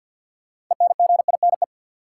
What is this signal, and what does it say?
Morse code, heard as a keyed tone, about 53 wpm, 700 Hz. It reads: ENGINE
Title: Sentences from Top 1500 Words - New Word Drill - 50wpm